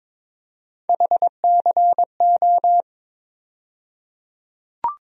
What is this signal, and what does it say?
Morse code, heard as a keyed tone, about 22 wpm, 700 Hz. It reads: HCO
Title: USA Call Sign Postfixes - 3 Letters 22wpm